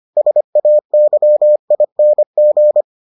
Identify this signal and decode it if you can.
Morse code, heard as a keyed tone, about 25 wpm, 600 Hz. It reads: SAYING